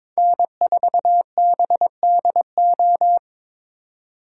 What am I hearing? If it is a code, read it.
Morse code, heard as a keyed tone, about 22 wpm, 700 Hz. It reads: N4BDO
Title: USA Call Signs - 4x Character Spacing - 22wpm